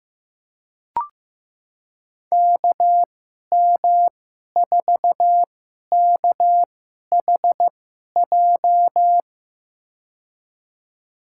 Morse code, heard as a keyed tone, about 15 wpm, spaced as for 11 wpm, 700 Hz. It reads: KM4KHJ
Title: USA Call Signs - 6 Characters - 2x Character Spacing - 15wpm